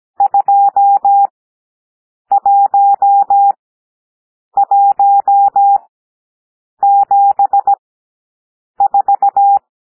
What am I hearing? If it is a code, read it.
Morse code, heard as a keyed tone, about 17 wpm, spaced as for 7 wpm, 800 Hz. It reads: 21174
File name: SOUND NUMBERS 17 wpm